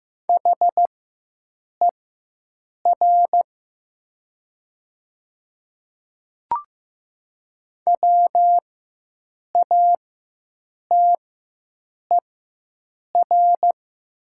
Morse code, heard as a keyed tone, about 15 wpm, spaced as for 7 wpm, 700 Hz. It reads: HER WATER
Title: 4x Character Spacing - Top 100 Words - 15wpm